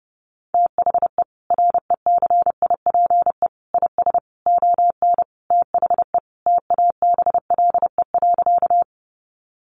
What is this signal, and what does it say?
Morse code, heard as a keyed tone, about 30 wpm, 700 Hz. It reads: THE RECIPE IS ON THE TABLE.
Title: Sentences from Top 1200 Words 30wpm